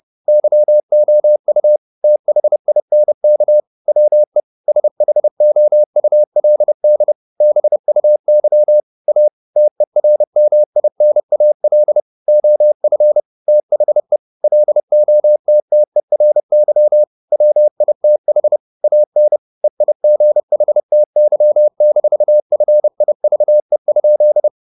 Morse code, heard as a keyed tone, about 30 wpm, 600 Hz. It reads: YOU THINK WE SHOULD BUY A TERMINAL OF THE LOTTERY WITH AN EIGHTY-FIVE?
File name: the old man and the sea / the old man and the sea 1 30wpm